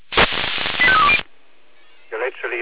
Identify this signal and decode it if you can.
Five-tone selective call (ZVEI), heard as quick runs of five tones, tones of about 70 ms, with speech. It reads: 85320